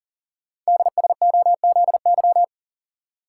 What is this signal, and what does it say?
Morse code, heard as a keyed tone, about 40 wpm, 700 Hz. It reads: DSOZY
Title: Code Groups - 5 Letters - 40wpm